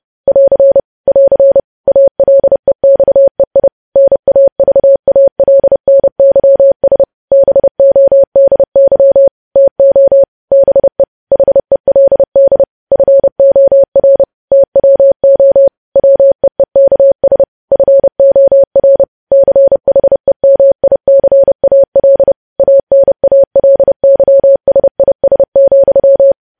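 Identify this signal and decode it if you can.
Morse code, heard as a keyed tone, about 30 wpm, 550 Hz. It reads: <AR> <AR> ALEXEI NAVALNYS BODY TO BE HELD FOR TWO WEEKS FOR CHEMICAL ANALYSIS,
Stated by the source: Morse code news headlines practice at 30 WPM.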